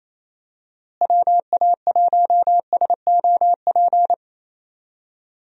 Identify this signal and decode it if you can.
Morse code, heard as a keyed tone, about 28 wpm, 700 Hz. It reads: WA1SOP
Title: USA Call Signs - 6 Characters - Repeated 2x - 28wpm